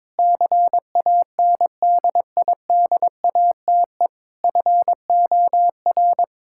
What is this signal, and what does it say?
Morse code, heard as a keyed tone, about 21 wpm, 700 Hz. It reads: CANDIDATE FOR
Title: Sentences from Top 1600 Words 22wpm